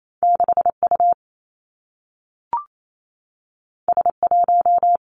Morse code, heard as a keyed tone, about 28 wpm, 700 Hz. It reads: 6 U S 1
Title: Two Challenging Letter-Number Combinations 28wpm